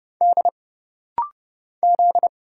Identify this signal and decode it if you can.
Morse code, heard as a keyed tone, about 30 wpm, 700 Hz. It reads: D Z
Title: Single Letters - Rapid-Fire 30wpm